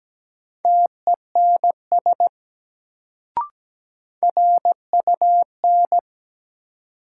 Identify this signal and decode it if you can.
Morse code, heard as a keyed tone, about 17 wpm, 700 Hz. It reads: TENS RUN
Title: Sets of 4 Words - New Word Drill - 17wpm